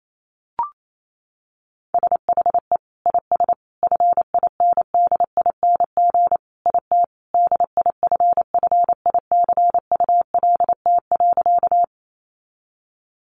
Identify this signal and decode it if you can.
Morse code, heard as a keyed tone, about 28 wpm, 700 Hz. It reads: SHE IS FINDING IT DIFFICULT.
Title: Sentences from Top 1100 Words 28wpm